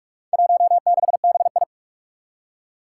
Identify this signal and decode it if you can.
Morse code, heard as a keyed tone, about 44 wpm, 700 Hz. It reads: 16BI
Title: Four Challenging Letter-Number Combinations 45wpm